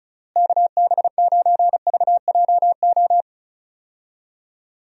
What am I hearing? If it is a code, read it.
Morse code, heard as a keyed tone, about 34 wpm, 700 Hz. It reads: KB9VJO
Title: USA Call Signs - 6 Characters - Repeated 3x - 35wpm